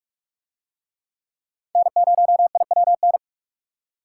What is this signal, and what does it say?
Morse code, heard as a keyed tone, about 40 wpm, 700 Hz. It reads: N0IWN